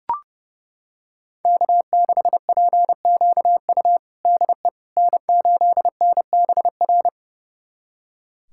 Morse code, heard as a keyed tone, about 29 wpm, 700 Hz. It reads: K6PQU DE N8NBR
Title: USA Callsign DE Callsign - 5 Characters - 30wpm